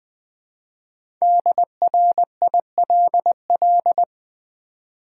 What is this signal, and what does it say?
Morse code, heard as a keyed tone, about 20 wpm, 690 Hz. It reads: DRILL
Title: Easy Words with Hints - 20wpm